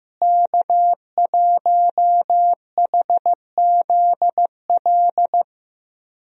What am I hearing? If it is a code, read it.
Morse code, heard as a keyed tone, about 15 wpm, 700 Hz. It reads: K1HZL
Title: USA Call Signs - 5 Characters - Repeated 3x - 15wpm